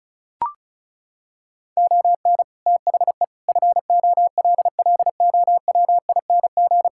Morse code, heard as a keyed tone, about 36 wpm, 700 Hz. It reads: ON THE FOLLOWING
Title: Sentences from Top 1000 Words - Review Part 6 - 35wpm